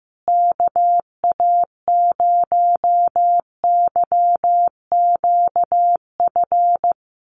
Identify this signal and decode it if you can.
Morse code, heard as a keyed tone, about 15 wpm, 700 Hz. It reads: KA0YQF
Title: USA Call Signs Repeated 2x 15wpm